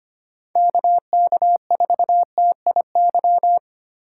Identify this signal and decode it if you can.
Morse code, heard as a keyed tone, about 25 wpm, 700 Hz. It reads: KK4TIY
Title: USA Call Signs - 6 Characters - 4x Character Spacing - 25wpm